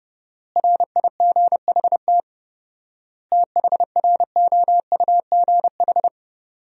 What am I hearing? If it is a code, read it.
Morse code, heard as a keyed tone, about 30 wpm, 700 Hz. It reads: RIGHT THROUGH
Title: 4x Word-Spacing - Sentences from Top 200 Words - 30wpm